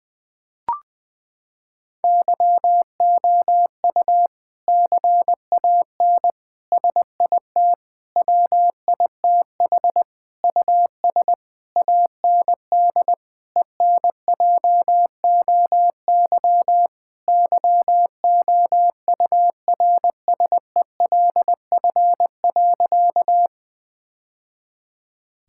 Morse code, heard as a keyed tone, about 21 wpm, 700 Hz. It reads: YOU CAN SIT WITH US AND ENJOY YOURSELF.
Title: Sentences from Top 1000 Words - Review Part 5 - 20wpm